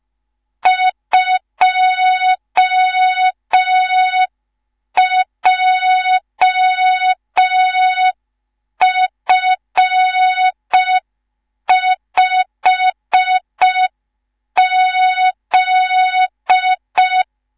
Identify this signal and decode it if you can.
Morse code, heard as a keyed tone, about 5 wpm, 3700 Hz. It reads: 2JF5Z
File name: Morze-HA2MN / hang / 19 GYAKORLAT